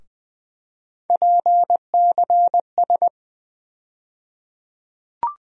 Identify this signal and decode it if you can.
Morse code, heard as a keyed tone, about 20 wpm, 700 Hz. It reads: PCS
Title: USA Call Sign Postfixes - 3 Letters 20wpm